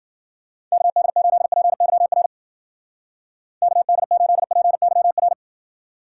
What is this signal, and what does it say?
Morse code, heard as a keyed tone, about 50 wpm, 700 Hz. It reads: KD8PYR KD8PYR
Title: Speed Racing - USA 2x3 Call Signs 40wpm